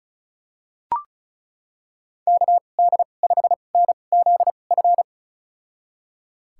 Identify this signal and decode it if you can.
Morse code, heard as a keyed tone, about 34 wpm, spaced as for 25 wpm, 700 Hz. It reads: KD5NZF
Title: USA Call Signs - 2x Character Spacing - 35wpm